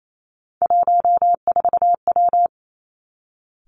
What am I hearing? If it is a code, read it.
Morse code, heard as a keyed tone, about 28 wpm, 700 Hz. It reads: 14W
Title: Three Challenging Letter-Number Combinations 28wpm